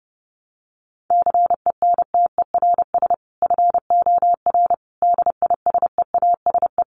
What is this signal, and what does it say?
Morse code, heard as a keyed tone, about 30 wpm, 700 Hz. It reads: CENTERS FOR DISEASE
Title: Sets of 5 Words Encore2 30wpm